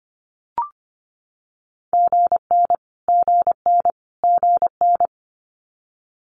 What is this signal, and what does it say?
Morse code, heard as a keyed tone, about 25 wpm, 700 Hz. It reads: GN GN GN
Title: Multi-letter Phonemes Repeated 3x 25wpm